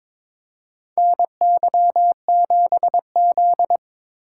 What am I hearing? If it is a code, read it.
Morse code, heard as a keyed tone, about 22 wpm, 700 Hz. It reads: NY7Z
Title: K1USN SST QSO Practice - 22wpm